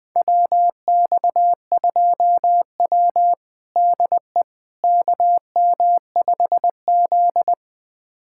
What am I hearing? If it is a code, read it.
Morse code, heard as a keyed tone, about 20 wpm, 700 Hz. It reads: WX2W DE KM5Z
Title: USA Callsign DE Callsign - 4 Characters - 20wpm